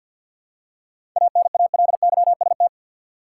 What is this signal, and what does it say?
Morse code, heard as a keyed tone, about 48 wpm, 700 Hz. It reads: ANALYST
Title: Seven Letter Words - 50wpm